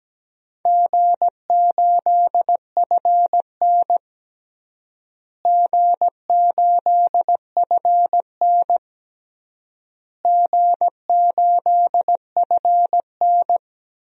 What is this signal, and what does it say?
Morse code, heard as a keyed tone, about 17 wpm, 700 Hz. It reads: G8FN G8FN G8FN
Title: DXCC-100 Call Signs - 4 Characters - Repeated 3x - 17wpm